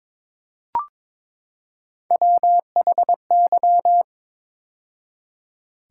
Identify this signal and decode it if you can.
Morse code, heard as a keyed tone, about 21 wpm, 700 Hz. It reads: WHY
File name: practice / 22wpm / Build A Phrase - 5-Word Phrases - 22wpm